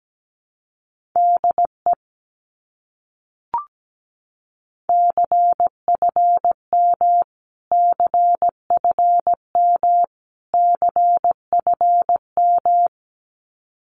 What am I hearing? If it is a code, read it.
Morse code, heard as a keyed tone, about 17 wpm, 700 Hz. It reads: DE CFM CFM CFM
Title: Abbreviations Repeated 3x - 17wpm